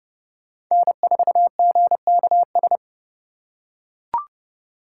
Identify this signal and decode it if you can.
Morse code, heard as a keyed tone, about 30 wpm, 700 Hz. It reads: N4GKS
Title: USA Call Signs - 4x Character Spacing - 30wpm